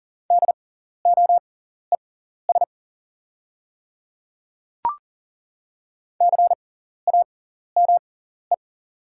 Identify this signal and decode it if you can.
Morse code, heard as a keyed tone, about 40 wpm, spaced as for 14 wpm, 690 Hz. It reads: DOES CAME
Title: 6x Character Spacing - Top 200 Words - 40wpm